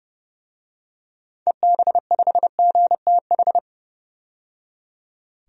Morse code, heard as a keyed tone, about 30 wpm, 700 Hz. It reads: EB5GTH
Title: DXCC-100 Call Signs - 6 Characters - 30wpm